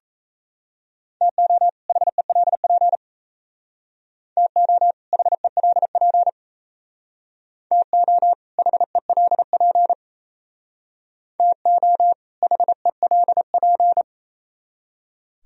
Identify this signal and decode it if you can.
Morse code, heard as a keyed tone, about 36 wpm, 700 Hz. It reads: TO HELP TO HELP TO HELP TO HELP